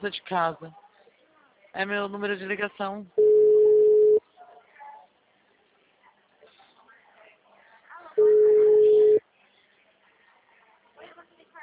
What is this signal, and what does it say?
Ringback tone (Europe, 425 Hz), with speech.
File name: out- -2011- - - .26